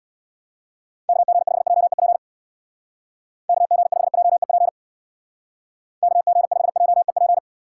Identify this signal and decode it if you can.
Morse code, heard as a keyed tone, about 59 wpm, 700 Hz. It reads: KG5JEP KG5JEP KG5JEP
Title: Speed Racing - USA Call Signs 50wpm